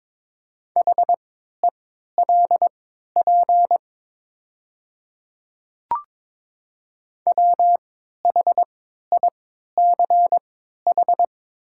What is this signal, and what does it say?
Morse code, heard as a keyed tone, about 22 wpm, spaced as for 13 wpm, 690 Hz. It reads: HELP WHICH
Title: 3x Character Spacing - Top 200 Words - 22wpm